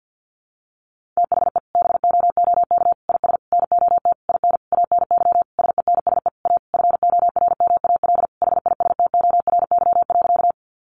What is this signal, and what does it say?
Morse code, heard as a keyed tone, about 48 wpm, 700 Hz. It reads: THE BOOK IS NOT IN ANY SENSE A FORMAL HISTORY.
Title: Sentences from Top 1300 Words 50wpm